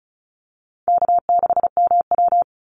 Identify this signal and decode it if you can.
Morse code, heard as a keyed tone, about 34 wpm, 700 Hz. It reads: K6MW
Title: NA SOTA Chaser Callsigns 35wpm